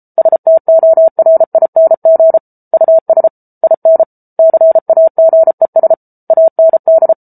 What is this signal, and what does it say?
Morse code, heard as a keyed tone, about 34 wpm, 650 Hz. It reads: STORING US IN CAGES AND